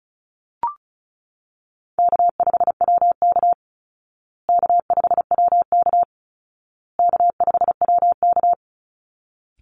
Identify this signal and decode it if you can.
Morse code, heard as a keyed tone, about 34 wpm, 700 Hz. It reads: K5WK K5WK K5WK
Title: Easy Call Signs Repeated 3x - 35wpm